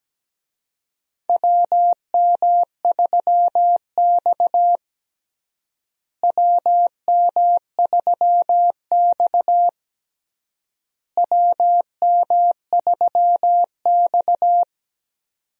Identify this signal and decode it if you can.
Morse code, heard as a keyed tone, about 17 wpm, 700 Hz. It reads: WM3X WM3X WM3X
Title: USA Call Signs - 4 Characters - Repeated 3x - 17wpm